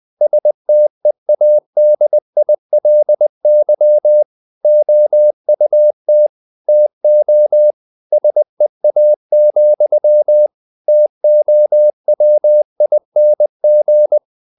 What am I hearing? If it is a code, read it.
Morse code, heard as a keyed tone, about 20 wpm, 600 Hz. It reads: STEADILY OUT TO SEA, TOWING
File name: the old man and the sea / the old man and the sea part 1 20wpm